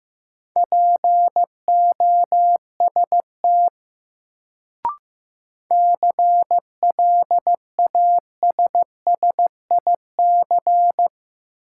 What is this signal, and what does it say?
Morse code, heard as a keyed tone, about 15 wpm, 700 Hz. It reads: POST CLASSIC
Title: Sentences from Top 1000 Words - New Word Drill - 15wpm